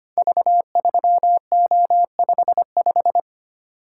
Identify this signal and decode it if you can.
Morse code, heard as a keyed tone, about 25 wpm, 700 Hz. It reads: V3O55
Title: Code Groups - 5 Letter:Numbers - 25wpm